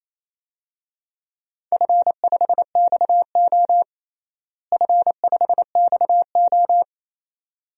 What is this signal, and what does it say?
Morse code, heard as a keyed tone, about 28 wpm, 700 Hz. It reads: F5XO F5XO
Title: DXCC-100 Call Signs - 4 Characters - Repeated 2x - 28wpm